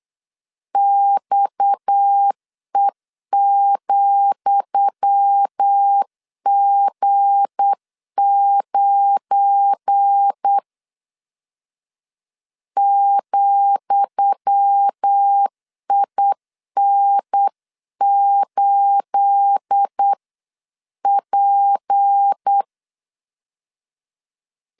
Morse code, heard as a keyed tone, about 8 wpm, 800 Hz. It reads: XE,G9 ,IN8P
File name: LEZIONI / 10 SIGNS / 10D COMA 8wpm 3749